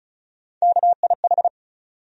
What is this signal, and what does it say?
Morse code, heard as a keyed tone, about 34 wpm, 700 Hz. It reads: KIH
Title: USA Call Sign Postfixes - 3 Letters 35wpm